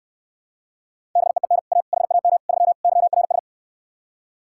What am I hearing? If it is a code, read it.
Morse code, heard as a keyed tone, about 72 wpm, 700 Hz. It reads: BEEN A HAM 3 YRS